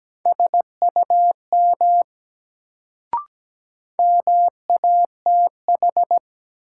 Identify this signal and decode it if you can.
Morse code, heard as a keyed tone, about 17 wpm, 700 Hz. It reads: SUM MATH
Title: Sentences from Top 1000 Words - Review Part 6 - New Word Drill - 17wpm